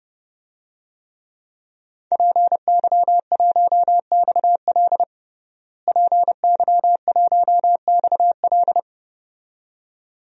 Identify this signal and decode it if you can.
Morse code, heard as a keyed tone, about 30 wpm, 700 Hz. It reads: PY1XL PY1XL
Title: DXCC-100 Call Signs - 5 Characters - Repeated 2x - 30wpm